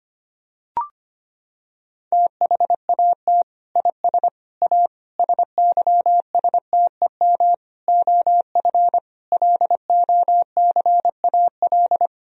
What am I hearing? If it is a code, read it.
Morse code, heard as a keyed tone, about 25 wpm, 700 Hz. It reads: THAT IS A SYSTEM OF LOCAL
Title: Sentences from Top 1000 Words - Review Part 3 - 25wpm